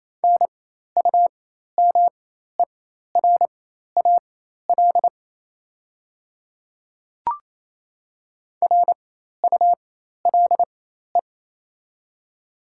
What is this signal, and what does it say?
Morse code, heard as a keyed tone, about 27 wpm, spaced as for 13 wpm, 700 Hz. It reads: NUMERAL RULE
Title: 4x Character Spacing - Top 500 Words - 28wpm